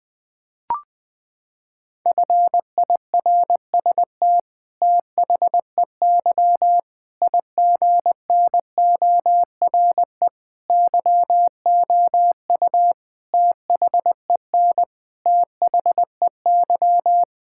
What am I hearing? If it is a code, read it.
Morse code, heard as a keyed tone, about 20 wpm, 700 Hz. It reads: FIRST THEY IGNORE YOU THEN THEY
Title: Sets of 8 Words - 20wpm